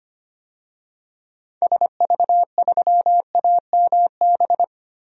Morse code, heard as a keyed tone, about 25 wpm, 700 Hz. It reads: SV3AMB
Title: DXCC-100 Call Signs - 6 Characters - 25wpm